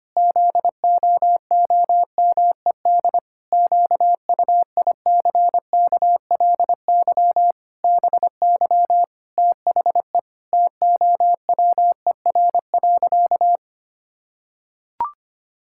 Morse code, heard as a keyed tone, about 25 wpm, 700 Hz. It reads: ZOOMED QUICKLY BY THE TOWER.
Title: Pangram Challenge v2 - 25wpm